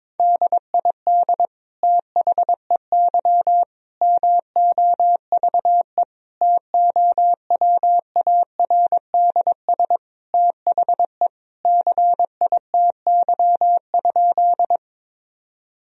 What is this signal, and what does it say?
Morse code, heard as a keyed tone, about 22 wpm, 700 Hz. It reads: DID THEY MOVE TOWARDS THE CITY?
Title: Sentences from Top 1000 Words - Review Part 2 - 22wpm